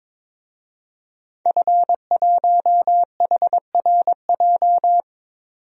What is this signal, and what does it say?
Morse code, heard as a keyed tone, about 22 wpm, 700 Hz. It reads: F1HRJ
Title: DXCC-100 Call Signs - 5 Characters - 22wpm